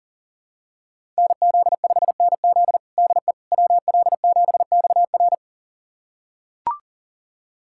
Morse code, heard as a keyed tone, about 40 wpm, 700 Hz. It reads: NZ5NZ DE WL7XR
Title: USA Callsign DE Callsign - 5 Characters - 40wpm